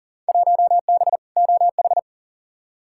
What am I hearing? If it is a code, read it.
Morse code, heard as a keyed tone, about 40 wpm, 700 Hz. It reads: 1B OH
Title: Field Day Exchanges - 40wpm